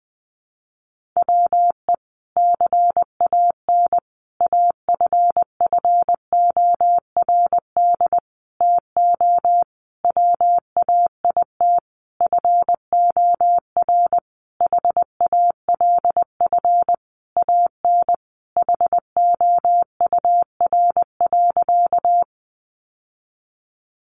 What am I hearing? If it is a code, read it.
Morse code, heard as a keyed tone, about 20 wpm, 700 Hz. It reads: WE CAN AFFORD TO WAIT FOR HALF AN HOUR.
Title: Sentences from Top 1100 Words 20wpm